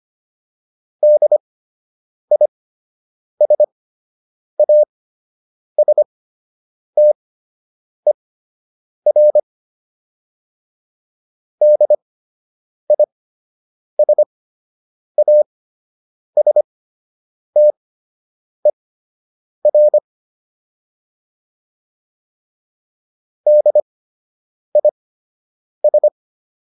Morse code, heard as a keyed tone, about 25 wpm, spaced as for 8 wpm, 600 Hz. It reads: DISASTER DISASTER DIS